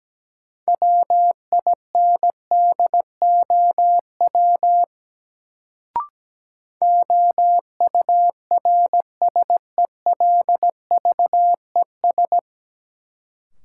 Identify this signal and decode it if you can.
Morse code, heard as a keyed tone, about 17 wpm, 700 Hz. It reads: WINDOW OURSELVES
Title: Sentences from Top 1600 Words - New Word Drill - 17wpm